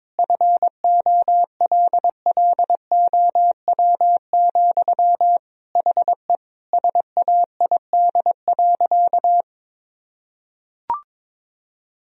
Morse code, heard as a keyed tone, about 22 wpm, 700 Hz. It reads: FOLLOW, HE SAID.
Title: Sentences from Top 1000 Words - Review Part 1 - 22wpm